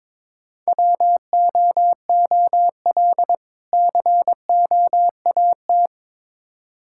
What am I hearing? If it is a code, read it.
Morse code, heard as a keyed tone, about 21 wpm, 700 Hz. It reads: WOOL COAT
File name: Difficult Words with Hints - 22wpm